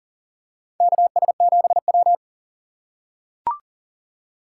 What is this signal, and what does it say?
Morse code, heard as a keyed tone, about 40 wpm, 700 Hz. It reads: KS7W
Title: USA Call Signs - 4 Characters - 2x Character Spacing - 40wpm